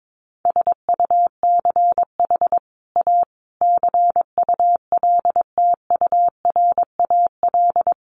Morse code, heard as a keyed tone, about 22 wpm, 700 Hz. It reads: SUCH A CULTURAL
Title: Sentences from Top 1500 Words 22wpm